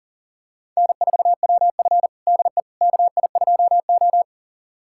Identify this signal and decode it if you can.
Morse code, heard as a keyed tone, about 37 wpm, 700 Hz. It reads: N4WF DE KI2O
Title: USA Callsign DE Callsign - 4 Characters - 40wpm